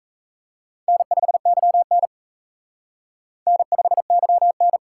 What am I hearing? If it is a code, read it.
Morse code, heard as a keyed tone, about 35 wpm, 700 Hz. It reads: N5YN N5YN
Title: Speed Racing - USA 1x2 Call Signs 28wpm